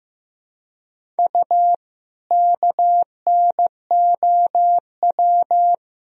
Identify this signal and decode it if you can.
Morse code, heard as a keyed tone, about 16 wpm, 700 Hz. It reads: U KNOW
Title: Sentences from Top 100 Words and Abbreviations 15wpm